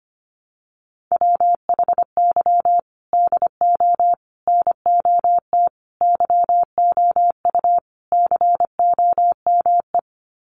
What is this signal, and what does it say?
Morse code, heard as a keyed tone, about 25 wpm, 700 Hz. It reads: WHY DO NOT YOU COME
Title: Sets of 5 Words Encore2 25wpm